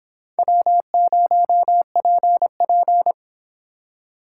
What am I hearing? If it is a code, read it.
Morse code, heard as a keyed tone, about 25 wpm, 700 Hz. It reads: W0PP